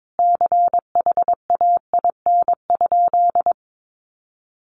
Morse code, heard as a keyed tone, about 22 wpm, 700 Hz. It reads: CHAIN?
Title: Sentences from Top 1500 Words 22wpm